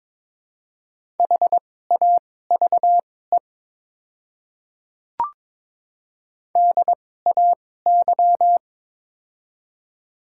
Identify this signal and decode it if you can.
Morse code, heard as a keyed tone, about 22 wpm, spaced as for 16 wpm, 700 Hz. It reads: HAVE DAY
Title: 2x Character Spacing - Top 100 Words - 22wpm